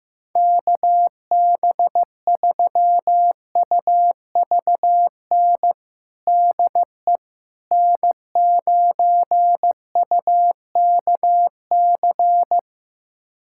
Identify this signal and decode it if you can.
Morse code, heard as a keyed tone, about 15 wpm, 700 Hz. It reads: KB3UVN DE N9UKC
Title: USA Callsign DE Callsign 15wpm